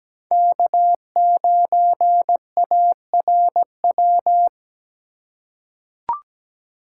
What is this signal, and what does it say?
Morse code, heard as a keyed tone, about 17 wpm, 700 Hz. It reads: K9ARW
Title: USA Call Signs - 5 Characters - 2x Character Spacing - 17wpm